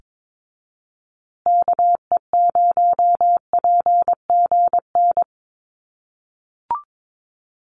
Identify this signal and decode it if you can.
Morse code, heard as a keyed tone, about 22 wpm, 700 Hz. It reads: KE0PGN
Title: USA Call Signs Repeated 3x 22wpm